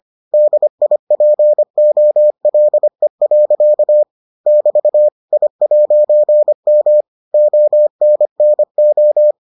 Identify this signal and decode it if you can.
Morse code, heard as a keyed tone, about 25 wpm, 600 Hz. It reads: DIPOLE. <BT> I'M ONNO